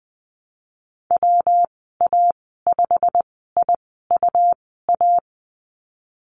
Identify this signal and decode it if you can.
Morse code, heard as a keyed tone, about 20 wpm, spaced as for 14 wpm, 700 Hz. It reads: WA5IUA
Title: 128 Lesson 21g - Number 6 - Call Signs 20wpm